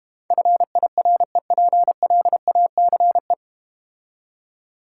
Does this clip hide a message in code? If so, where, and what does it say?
Yes: Morse code from 0.3 to 3.3 s, heard as a keyed tone, about 31 wpm, 700 Hz. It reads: FIREPLACE